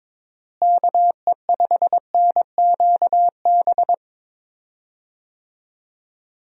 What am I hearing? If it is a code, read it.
Morse code, heard as a keyed tone, about 22 wpm, 700 Hz. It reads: KE5NQB
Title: USA Call Signs - 6 Characters - Repeated 2x - 22wpm